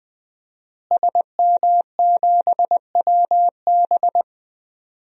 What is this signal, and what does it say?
Morse code, heard as a keyed tone, about 20 wpm, 700 Hz. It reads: SM7WB
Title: DXCC-100 Call Signs - 5 Characters - Repeated 2x - 20wpm